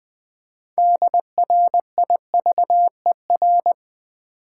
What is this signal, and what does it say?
Morse code, heard as a keyed tone, about 20 wpm, 700 Hz. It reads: DRIVER
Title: Six Letter Words - 20wpm